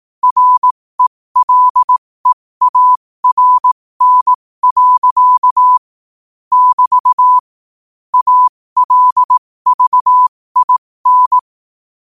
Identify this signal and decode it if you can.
Morse code, heard as a keyed tone, about 18 wpm, spaced as for 15 wpm, 1000 Hz. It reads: RELEARN. <BT> ALVIN
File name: qotdfiles / 15wpm